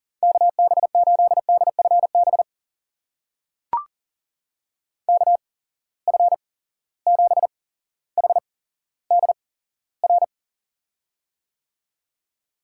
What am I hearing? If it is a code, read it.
Morse code, heard as a keyed tone, about 40 wpm, spaced as for 17 wpm, 700 Hz. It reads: KB8DFB KF7HDR
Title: USA Call Signs - 6 Characters - 8x Character Spacing - 40wpm